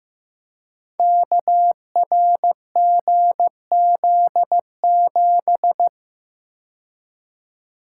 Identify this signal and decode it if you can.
Morse code, heard as a keyed tone, about 15 wpm, 700 Hz. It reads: KRGZ7
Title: Code Groups - 5 Letter:Numbers - 15wpm